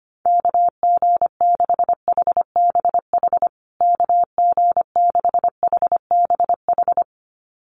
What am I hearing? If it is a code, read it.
Morse code, heard as a keyed tone, about 25 wpm, 700 Hz. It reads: KG6HBH KG6HBH
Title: USA Call Signs Repeated 2x 25wpm